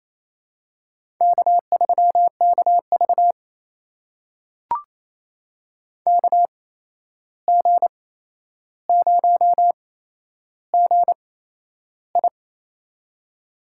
Morse code, heard as a keyed tone, about 28 wpm, spaced as for 11 wpm, 700 Hz. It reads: K3KV KG0GI
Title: USA Call Signs - 8x Character Spacing - 28wpm